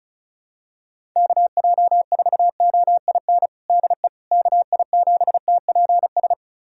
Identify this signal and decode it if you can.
Morse code, heard as a keyed tone, about 36 wpm, 700 Hz. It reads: KJ4OIN DE KI7TPS
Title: USA Callsign DE Callsign 35wpm